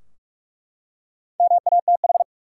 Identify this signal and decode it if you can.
Morse code, heard as a keyed tone, about 47 wpm, 700 Hz. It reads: MATH
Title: Four Letter Words - 45wpm